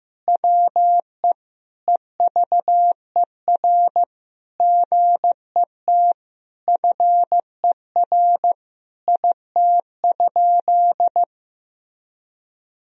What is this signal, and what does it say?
Morse code, heard as a keyed tone, about 15 wpm, 700 Hz. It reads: WE EVER GET FER IT?
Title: Sentences from Top 400 Words and Abbreviations 15wpm